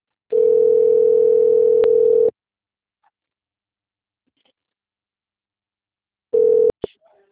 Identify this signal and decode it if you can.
Ringback tone (North America), 440 + 480 Hz.